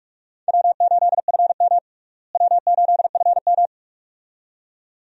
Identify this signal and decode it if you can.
Morse code, heard as a keyed tone, about 42 wpm, 700 Hz. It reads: W8FM W8FM
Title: USA Call Signs - 4 Characters - Repeated 2x - 45wpm